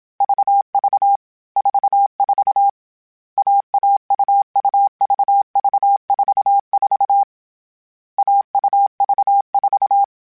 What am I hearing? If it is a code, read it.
Morse code, heard as a keyed tone, about 27 wpm, 800 Hz. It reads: VV 44 AAUUVV44 AUV4
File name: Short Runs / Short Runs-mp3s / AUV AUV4 @25